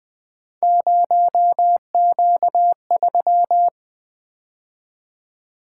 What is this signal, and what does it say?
Morse code, heard as a keyed tone, about 20 wpm, 700 Hz. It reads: 0Q3
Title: Three Letter-Number Combinations - 20wpm